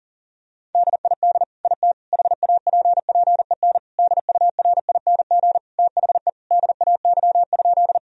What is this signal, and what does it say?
Morse code, heard as a keyed tone, about 40 wpm, 700 Hz. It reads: DID IT HAPPEN DURING THE DAY?
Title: Sentences from Top 1000 Words - Review Part 2 - 40wpm